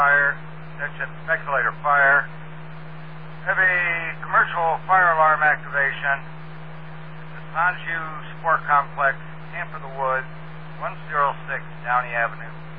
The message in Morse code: none

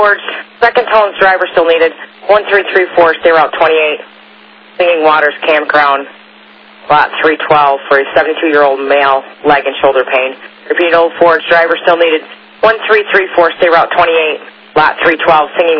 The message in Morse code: none

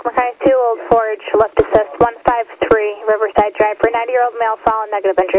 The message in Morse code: none

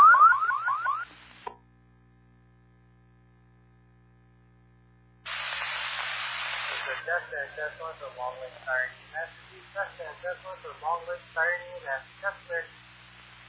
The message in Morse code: none

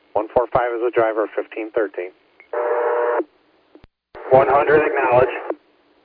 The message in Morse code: none

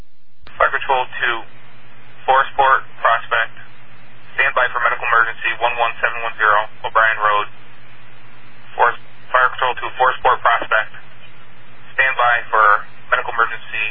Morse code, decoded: none

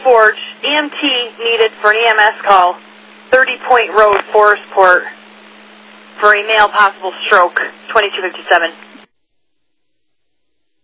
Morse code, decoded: none

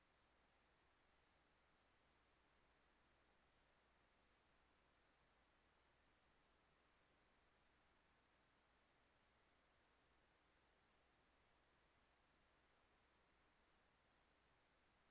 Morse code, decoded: none

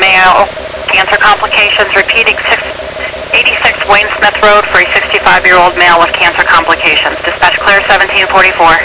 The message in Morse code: none

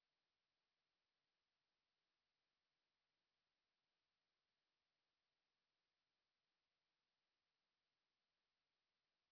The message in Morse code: none

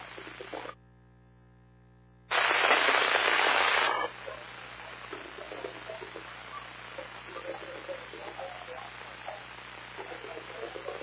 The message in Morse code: none

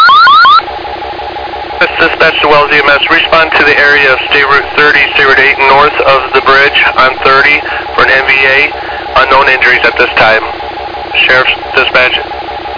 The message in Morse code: none

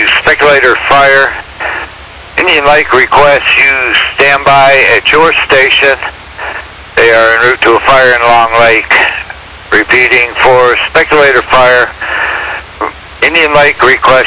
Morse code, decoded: none

